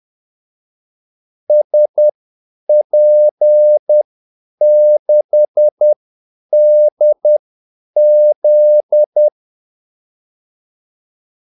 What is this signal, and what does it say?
Morse code, heard as a keyed tone, about 10 wpm, spaced as for 8 wpm, 600 Hz. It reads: SP6DZ
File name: Callsigns-10-8wpm